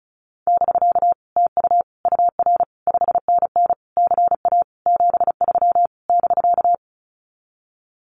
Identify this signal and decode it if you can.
Morse code, heard as a keyed tone, about 36 wpm, 700 Hz. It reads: <BK> TU UR 5NN CA 73 <BK>
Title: Common QSO Phrases Encore 35wpm